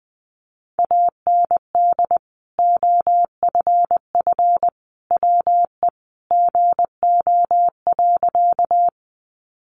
Morse code, heard as a keyed tone, about 20 wpm, 700 Hz. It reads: AND OFF WE GO.
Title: Sentences from Top 200 Words 20wpm